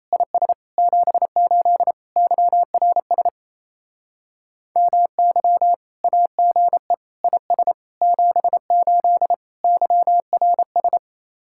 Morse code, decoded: IS 78 YRS MY AGE IS 78 YRS

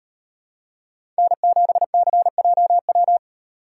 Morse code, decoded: N7CJW

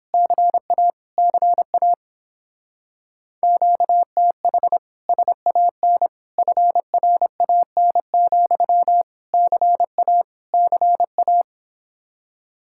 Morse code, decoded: CA CA QTH SAN FRAN, CA CA

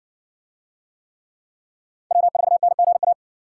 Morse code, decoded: W4NGA

